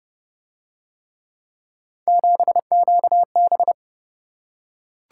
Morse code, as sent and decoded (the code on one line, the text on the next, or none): --... --.- -...
7QB